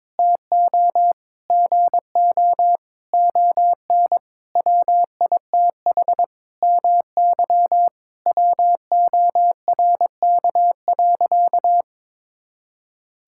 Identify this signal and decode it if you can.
Morse code, heard as a keyed tone, about 21 wpm, 700 Hz. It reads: TO GO ON WITH MY WORK.
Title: Sentences from Top 1000 Words - Review Part 6 - 22wpm